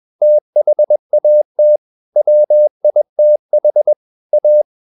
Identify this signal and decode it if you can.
Morse code, heard as a keyed tone, about 21 wpm, 600 Hz. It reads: THAT WITH A
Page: CW med Gnister